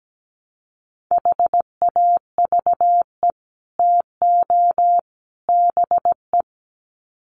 Morse code, heard as a keyed tone, about 17 wpm, 700 Hz. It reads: HAVE TO BE
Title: Sets of 3 Words 17wpm